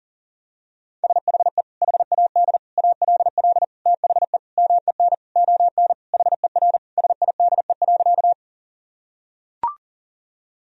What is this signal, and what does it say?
Morse code, heard as a keyed tone, about 40 wpm, 700 Hz. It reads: SHE HAD ALL THE MEN ON HER SIDE.
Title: Sentences from Top 1000 Words - Review Part 1 - 40wpm